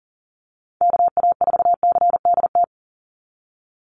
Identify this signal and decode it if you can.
Morse code, heard as a keyed tone, about 40 wpm, 700 Hz. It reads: KA4CDT
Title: USA Call Signs 40wpm